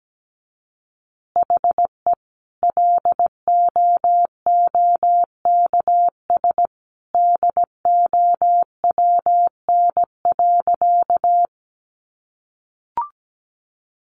Morse code, heard as a keyed tone, about 17 wpm, 700 Hz. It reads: HE LOOKS DOWN.
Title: Additional Easy Phrases 17wpm